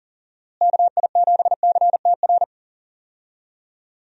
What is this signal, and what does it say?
Morse code, heard as a keyed tone, about 40 wpm, 700 Hz. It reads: KI7CTR
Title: USA Call Signs - 6 Characters - Repeated 2x - 40wpm